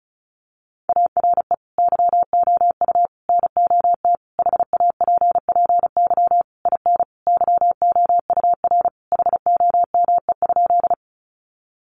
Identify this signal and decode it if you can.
Morse code, heard as a keyed tone, about 34 wpm, 700 Hz. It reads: ARE YOU NOT HAPPY IN YOUR HOME?
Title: Sentences from Top 1000 Words 35wpm